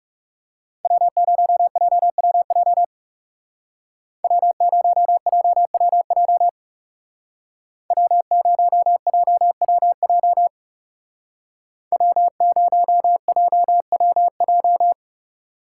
Morse code, heard as a keyed tone, about 36 wpm, 700 Hz. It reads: W0JWJ W0JWJ W0JWJ W0JWJ
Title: Speed Racing - USA 1x3 Call Signs 30wpm